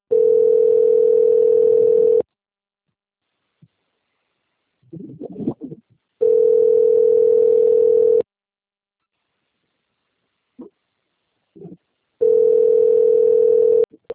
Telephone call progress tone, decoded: ringback tone

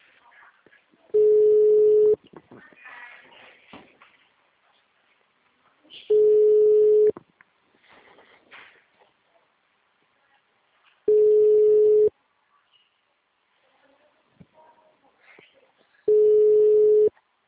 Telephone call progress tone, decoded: ringback tone

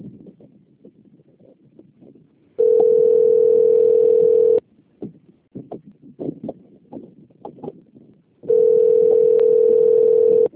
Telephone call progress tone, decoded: ringback tone

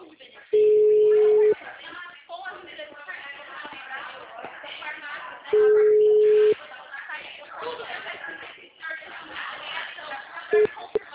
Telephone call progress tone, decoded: ringback tone

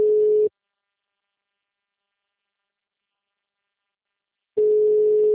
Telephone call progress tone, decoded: ringback tone

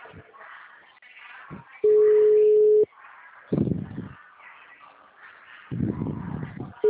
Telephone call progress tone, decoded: ringback tone

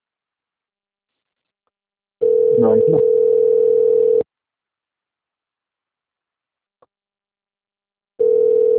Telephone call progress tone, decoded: ringback tone

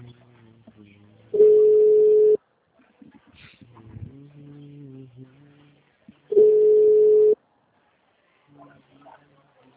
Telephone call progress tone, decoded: ringback tone

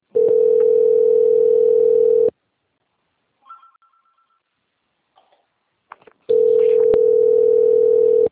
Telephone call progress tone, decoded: ringback tone

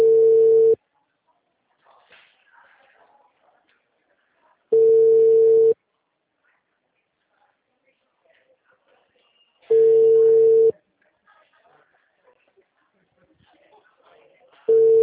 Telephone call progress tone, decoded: ringback tone

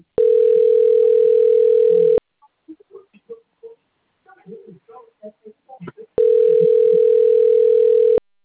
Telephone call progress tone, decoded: ringback tone

